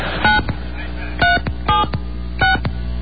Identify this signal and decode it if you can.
DTMF, heard as telephone touch tones, tones of about 160 ms. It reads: 9376
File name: mp3files / off the hook